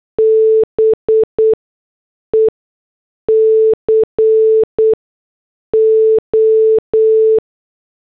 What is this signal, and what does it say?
Morse code, heard as a keyed tone, about 8 wpm, spaced as for 6 wpm, 440 Hz. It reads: BECO
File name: morse / headlines